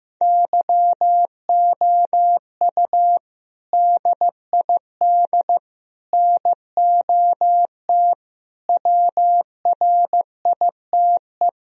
Morse code, heard as a keyed tone, about 15 wpm, 700 Hz. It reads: YOU DID NOT WRITE
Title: Sentences from Top 1000 Words - Review Part 1 - 15wpm